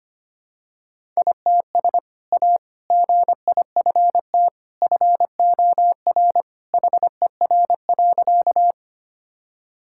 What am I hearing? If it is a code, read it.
Morse code, heard as a keyed tone, about 25 wpm, 700 Hz. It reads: ITS A GIFT FOR HER.
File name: practice / 25wpm / Sentences with 2-Word Phrases - 25wpm